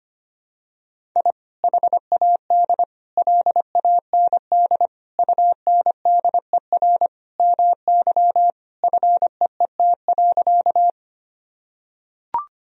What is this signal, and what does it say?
Morse code, heard as a keyed tone, about 25 wpm, 700 Hz. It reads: I HAD LAND UNDER MY FEET.
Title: Sentences from Top 1000 Words - Review Part 1 - 25wpm